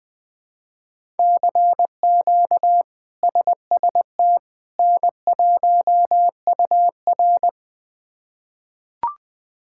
Morse code, heard as a keyed tone, about 21 wpm, 700 Hz. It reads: CQ SST N1UR
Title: K1USN SST QSO Practice - 20wpm